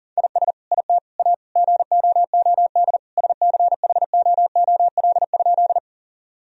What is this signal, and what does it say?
Morse code, heard as a keyed tone, about 40 wpm, 700 Hz. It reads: IS IT A GOOD SCHOOL?
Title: Sentences from Top 1000 Words - Review Part 1 - 40wpm